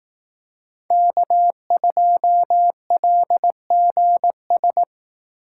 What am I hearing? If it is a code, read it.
Morse code, heard as a keyed tone, about 18 wpm, 700 Hz. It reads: K2LGS
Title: Speed Racing - USA 1x3 Call Signs 15wpm